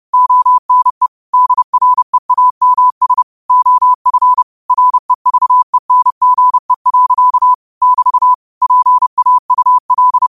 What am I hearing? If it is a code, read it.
Morse code, heard as a keyed tone, about 30 wpm, 1000 Hz. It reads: ONE DREAMS OF REVENGE. <BT> PAUL
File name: qotdfiles / 30wpm